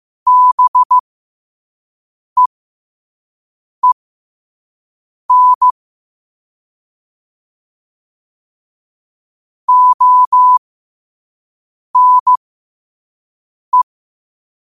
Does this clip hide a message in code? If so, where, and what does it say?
Yes: Morse code from 0.3 to 13.8 s, heard as a keyed tone, about 15 wpm, spaced as for 5 wpm, 1000 Hz. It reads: BEEN ONE